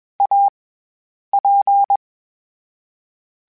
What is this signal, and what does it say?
Morse code, heard as a keyed tone, about 21 wpm, 800 Hz. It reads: A P
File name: Variable Speed / Variable Speed-mp3s / T AOEI's sp @20